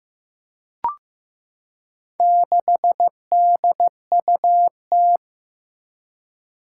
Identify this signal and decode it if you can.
Morse code, heard as a keyed tone, about 15 wpm, 700 Hz. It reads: 6DUT
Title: Four Challenging Letter-Number Combinations 15wpm